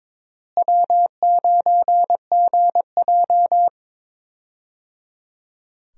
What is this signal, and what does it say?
Morse code, heard as a keyed tone, about 22 wpm, 700 Hz. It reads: W9GJ
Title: USA Call Signs - 4 Characters - Repeated 3x - 22wpm